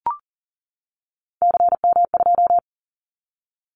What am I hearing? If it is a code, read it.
Morse code, heard as a keyed tone, about 40 wpm, 700 Hz. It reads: CM2
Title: Callsign Prefixes for Top 100 DXCC - 40wpm